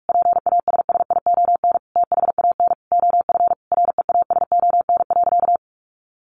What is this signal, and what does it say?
Morse code, heard as a keyed tone, about 44 wpm, 700 Hz. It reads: PASSION THAN OF REASON.
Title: Sentences from Top 1300 Words 45wpm